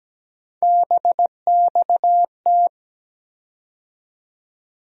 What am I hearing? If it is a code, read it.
Morse code, heard as a keyed tone, about 17 wpm, 700 Hz. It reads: BXT
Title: USA Call Sign Postfixes - 3 Letters 17wpm